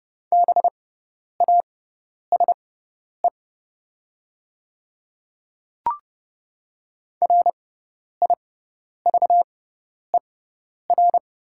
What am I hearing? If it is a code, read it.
Morse code, heard as a keyed tone, about 30 wpm, spaced as for 10 wpm, 700 Hz. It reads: BASE RIVER
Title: 6x Character Spacing - Top 300 Words - 30wpm